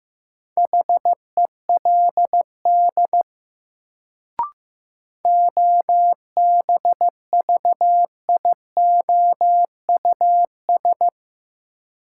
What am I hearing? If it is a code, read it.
Morse code, heard as a keyed tone, about 15 wpm, 700 Hz. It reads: HELD OBVIOUS
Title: Sentences from Top 1600 Words - New Word Drill - 15wpm